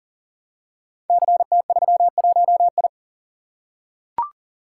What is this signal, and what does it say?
Morse code, heard as a keyed tone, about 42 wpm, 690 Hz. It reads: CT31I
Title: Code Groups - 5 Letter:Numbers - 40wpm